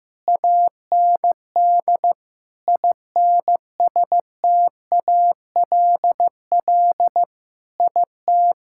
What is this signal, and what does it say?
Morse code, heard as a keyed tone, about 15 wpm, 700 Hz. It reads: AND INSTALL IT
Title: Sentences from Top 1600 Words 15wpm